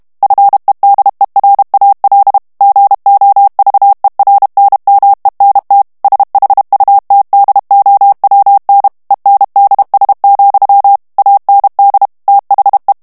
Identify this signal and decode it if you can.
Morse code, heard as a keyed tone, about 32 wpm, 800 Hz. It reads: FEDERAL GOVERNMENT SHUTDOWN ENDS, AND THE
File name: ANGLAIS texte clair 3 vitesse 30 mots minute